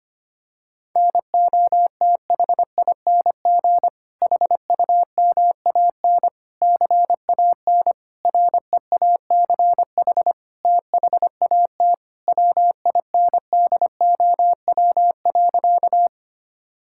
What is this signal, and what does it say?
Morse code, heard as a keyed tone, about 25 wpm, 700 Hz. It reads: NOTHING HUMAN CAN REACH THAT WINDOW.
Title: Sentences from Top 1600 Words 25wpm